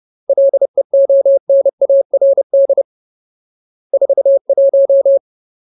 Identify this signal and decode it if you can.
Morse code, heard as a keyed tone, about 30 wpm, 550 Hz. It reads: LEONARD 41
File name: Ultimate Word List / 30WPM Ultimate